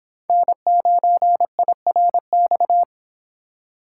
Morse code, heard as a keyed tone, about 25 wpm, 700 Hz. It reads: N9IRX